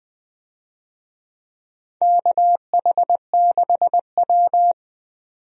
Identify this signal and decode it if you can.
Morse code, heard as a keyed tone, about 20 wpm, 700 Hz. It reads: KH6W
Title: DXCC-100 Call Signs - 4 Characters - 20wpm